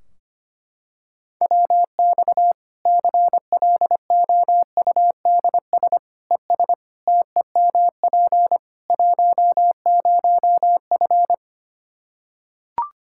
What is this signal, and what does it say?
Morse code, heard as a keyed tone, about 25 wpm, 700 Hz. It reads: WX CLOUDS ES TEMP 10F
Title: Practice QSO - Exchange 2 - Info - 25wpm